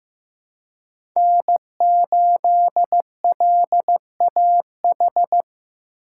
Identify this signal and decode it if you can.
Morse code, heard as a keyed tone, about 15 wpm, 700 Hz. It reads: N8LAH
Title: USA Call Signs - 5 Characters - 3x Character Spacing - 15wpm